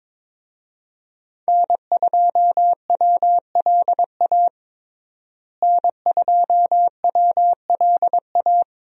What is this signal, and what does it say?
Morse code, heard as a keyed tone, about 22 wpm, 700 Hz. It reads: N2WLA N2WLA
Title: USA Call Signs - 5 Characters - Repeated 2x - 22wpm